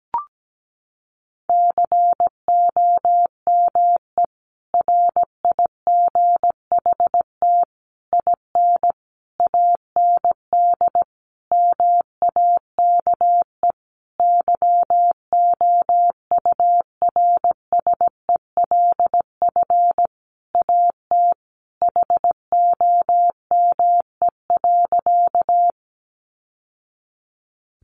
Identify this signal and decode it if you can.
Morse code, heard as a keyed tone, about 17 wpm, 700 Hz. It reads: COME RIGHT IN AND MAKE YOURSELF AT HOME.
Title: Sentences from Top 600 Words 17wpm